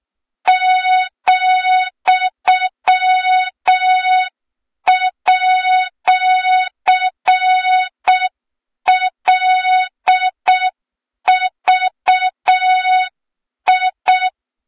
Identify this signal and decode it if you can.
Morse code, heard as a keyed tone, about 6 wpm, 3700 Hz. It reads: ,@LVI
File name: Morze-HA2MN / hang / 23 GYAKORLAT